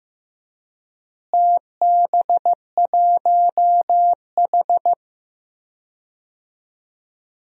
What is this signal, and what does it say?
Morse code, heard as a keyed tone, about 15 wpm, 700 Hz. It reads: TB1H
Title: DXCC-100 Call Signs - 4 Characters - Repeated 3x - 15wpm